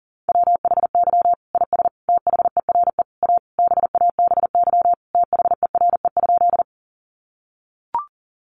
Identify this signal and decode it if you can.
Morse code, heard as a keyed tone, about 40 wpm, 700 Hz. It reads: WHY IS THERE A BABY THERE?
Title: Sentences from Top 1300 Words 40wpm